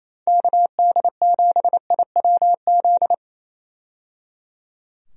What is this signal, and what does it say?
Morse code, heard as a keyed tone, about 28 wpm, 700 Hz. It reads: KD7IWZ
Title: USA Call Signs - 6 Characters - 28wpm